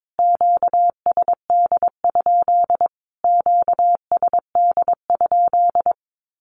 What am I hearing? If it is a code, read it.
Morse code, heard as a keyed tone, about 22 wpm, 700 Hz. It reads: QSD? QSD?
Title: Q-Codes Repeated 3x 22wpm